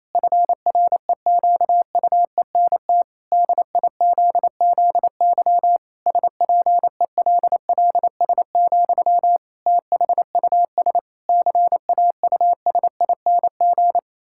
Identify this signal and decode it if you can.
Morse code, heard as a keyed tone, about 28 wpm, 700 Hz. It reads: FREQUENT DIZZY SPELLS, THUS CAUSING
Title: Pangram Challenge v2 - 28wpm